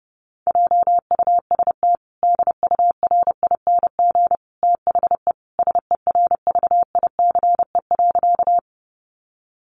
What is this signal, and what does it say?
Morse code, heard as a keyed tone, about 30 wpm, 700 Hz. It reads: JUST DURING THE SERVICE.
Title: Sentences from Top 700 Words 30wpm